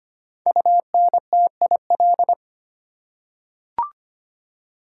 Morse code, heard as a keyed tone, about 25 wpm, 700 Hz. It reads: UNTIL
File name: 5 Letter Words - First 3 Letters Given - 25wpm